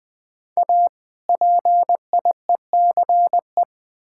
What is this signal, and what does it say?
Morse code, heard as a keyed tone, about 21 wpm, 700 Hz. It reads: A PIECE
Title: Build A Phrase - 3-Word Phrases - 20wpm